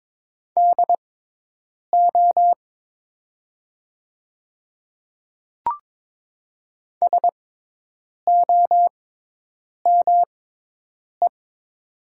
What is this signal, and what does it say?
Morse code, heard as a keyed tone, about 22 wpm, spaced as for 8 wpm, 700 Hz. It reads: DO SOME
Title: 6x Character Spacing - Top 100 Words - 22wpm